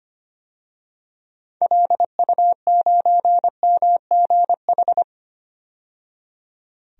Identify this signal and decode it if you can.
Morse code, heard as a keyed tone, about 25 wpm, 700 Hz. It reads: LU9MGH
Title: DXCC-100 Call Signs - 6 Characters - 25wpm